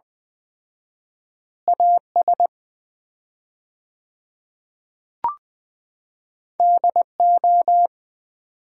Morse code, heard as a keyed tone, about 20 wpm, 700 Hz. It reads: AS DO
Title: 4x Word-Spacing - Sets of 3 Words - 20wpm